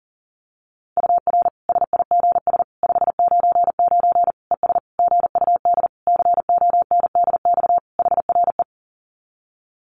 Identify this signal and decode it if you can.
Morse code, heard as a keyed tone, about 40 wpm, 700 Hz. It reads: UR SIGS 599 ES GUD CONDX HRE
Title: Common QSO Phrases 40wpm